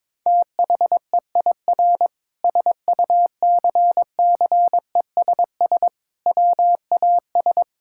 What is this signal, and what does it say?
Morse code, heard as a keyed tone, about 22 wpm, 700 Hz. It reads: THEIR SUCCESS WAS
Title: Sentences from Top 1000 Words - Review Part 6 - 22wpm